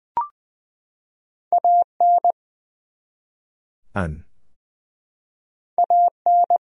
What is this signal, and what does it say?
Morse code, heard as a keyed tone, about 20 wpm, 700 Hz. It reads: AN AN
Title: Top 100 Words 20wpm